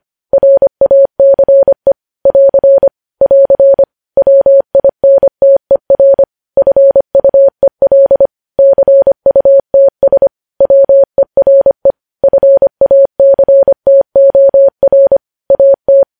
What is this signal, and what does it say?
Morse code, heard as a keyed tone, about 25 wpm, 550 Hz. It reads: RACE <AR> <AR> WINTER FUEL CUTS WERE FACTOR AT
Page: news headlines in morse code at 25 wpm